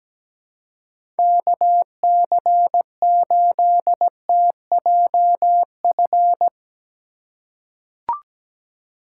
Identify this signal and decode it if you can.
Morse code, heard as a keyed tone, about 17 wpm, 700 Hz. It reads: KC8TJF
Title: USA Call Signs - 6 Characters - 3x Character Spacing - 17wpm